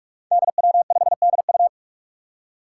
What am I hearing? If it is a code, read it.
Morse code, heard as a keyed tone, about 46 wpm, 700 Hz. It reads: DW5DU
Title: Five Challenging Letter-Number Combinations 45wpm